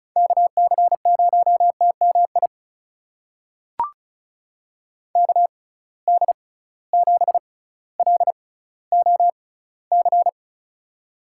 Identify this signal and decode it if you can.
Morse code, heard as a keyed tone, about 36 wpm, spaced as for 18 wpm, 700 Hz. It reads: KC0TMI KD7LOC
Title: USA Call Signs - 6 Characters - 6x Character Spacing - 35wpm